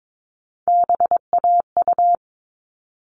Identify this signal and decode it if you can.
Morse code, heard as a keyed tone, about 22 wpm, 700 Hz. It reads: BAU
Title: All Cut Numbers - 3-Digit 22wpm